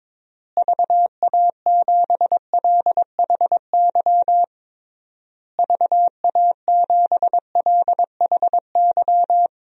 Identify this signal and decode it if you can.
Morse code, heard as a keyed tone, about 22 wpm, 700 Hz. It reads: VA7LHY VA7LHY
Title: DXCC-100 Call Signs - 6 Characters - Repeated 3x - 22wpm